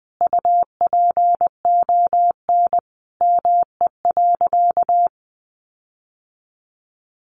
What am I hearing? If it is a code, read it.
Morse code, heard as a keyed tone, about 20 wpm, 700 Hz. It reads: UPON ME.
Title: Sentences from Top 1500 Words 20wpm